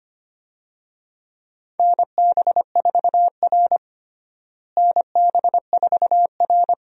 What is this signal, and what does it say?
Morse code, heard as a keyed tone, about 25 wpm, 700 Hz. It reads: NB4R NB4R
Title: USA Call Signs - 4 Characters - Repeated 2x - 25wpm